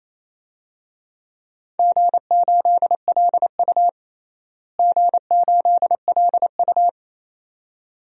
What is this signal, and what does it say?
Morse code, heard as a keyed tone, about 28 wpm, 700 Hz. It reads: G8LU G8LU
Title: DXCC-100 Call Signs - 4 Characters - Repeated 2x - 28wpm